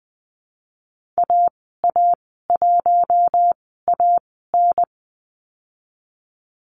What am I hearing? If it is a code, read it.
Morse code, heard as a keyed tone, about 19 wpm, spaced as for 14 wpm, 700 Hz. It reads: AA1AN
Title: 048 Lesson 9g - Number 4 - Call Signs 20wpm